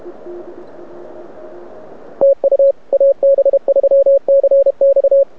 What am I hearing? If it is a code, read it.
Morse code, heard as a keyed tone, about 29 wpm, 550 Hz. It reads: TU AB3CX
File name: rec contest / arrl cw 2018 / test 32